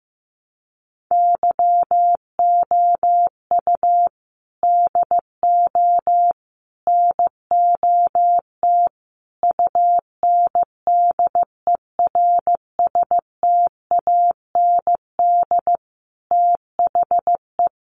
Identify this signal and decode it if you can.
Morse code, heard as a keyed tone, about 15 wpm, 700 Hz. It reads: YOU DO NOT UNDERSTAND THE
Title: Sentences from Top 600 Words 15wpm